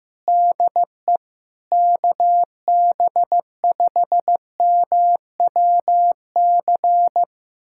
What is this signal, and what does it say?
Morse code, heard as a keyed tone, about 15 wpm, 700 Hz. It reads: DE KB5MWC
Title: USA Callsign DE Callsign - 6 Characters - 15wpm